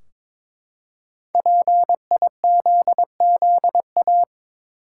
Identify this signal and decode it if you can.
Morse code, heard as a keyed tone, about 22 wpm, 700 Hz. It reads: PIZZA
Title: Five Letter Words - 22wpm